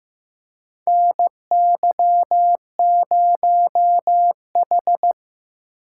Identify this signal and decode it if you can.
Morse code, heard as a keyed tone, about 15 wpm, 700 Hz. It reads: NY0H
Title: USA Call Signs - 4 Characters - 2x Character Spacing - 15wpm